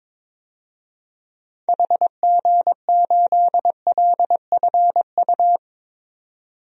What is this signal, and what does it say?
Morse code, heard as a keyed tone, about 22 wpm, 700 Hz. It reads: HG8LFU
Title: DXCC-100 Call Signs - 6 Characters - 22wpm